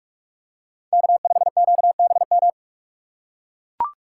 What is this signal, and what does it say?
Morse code, heard as a keyed tone, about 44 wpm, 700 Hz. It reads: K5QBM
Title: USA Call Signs - 5 Characters - 6x Character Spacing - 45wpm